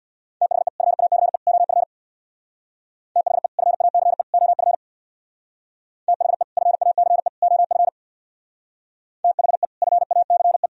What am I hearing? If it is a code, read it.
Morse code, heard as a keyed tone, about 61 wpm, 700 Hz. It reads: THE FACE OF THE FACE OF THE FACE OF THE FACE
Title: Speed-Racing - Sets of 3 Words 50wpm